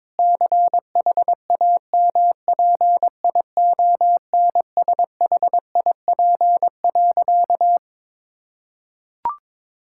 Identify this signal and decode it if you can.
Morse code, heard as a keyed tone, about 22 wpm, 700 Hz. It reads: CHAMPIONSHIP.
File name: practice / 22wpm / Sentences with 8-Word Phrases - 22wpm